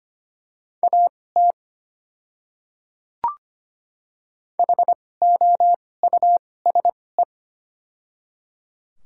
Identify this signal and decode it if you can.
Morse code, heard as a keyed tone, about 25 wpm, spaced as for 18 wpm, 700 Hz. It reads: AT HOUSE
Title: 2x Character Spacing - Top 200 Words - 25wpm